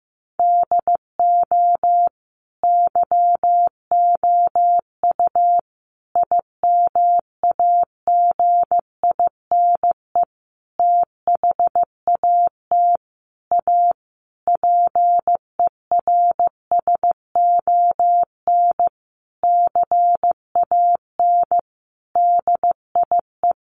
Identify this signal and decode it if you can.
Morse code, heard as a keyed tone, about 15 wpm, 700 Hz. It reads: DO YOU IMAGINE THAT A PERSON CAN DIE
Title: Sentences from Top 1400 Words 15wpm